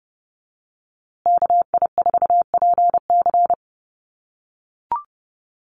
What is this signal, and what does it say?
Morse code, heard as a keyed tone, about 30 wpm, 700 Hz. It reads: KI4PC
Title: USA Call Signs Repeated 3x 30wpm